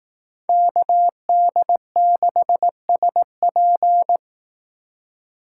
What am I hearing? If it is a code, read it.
Morse code, heard as a keyed tone, about 18 wpm, 700 Hz. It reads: KD6SP
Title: Speed Racing - USA 2x2 Call Signs 15wpm